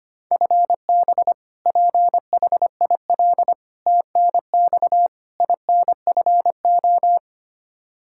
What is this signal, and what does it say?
Morse code, heard as a keyed tone, about 25 wpm, 700 Hz. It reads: FB PHIL TNX INFO
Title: Practice QSO - Exchange 3 - More Info - 25wpm